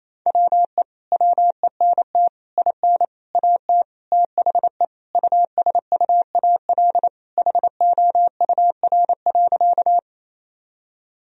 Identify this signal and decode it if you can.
Morse code, heard as a keyed tone, about 28 wpm, 700 Hz. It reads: WE WENT IN AT THE USUAL HOUR.
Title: Sentences from Top 1000 Words - Review Part 2 - 28wpm